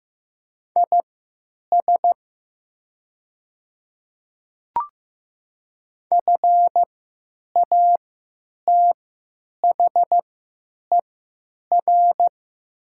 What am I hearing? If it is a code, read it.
Morse code, heard as a keyed tone, about 15 wpm, spaced as for 9 wpm, 700 Hz. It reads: IS FATHER
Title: 3x Character Spacing - Top 200 Words - 15wpm